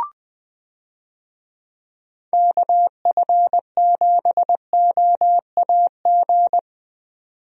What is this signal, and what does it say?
Morse code, heard as a keyed tone, about 20 wpm, 700 Hz. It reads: KF7OAG
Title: USA Call Signs - 6 Characters - 20wpm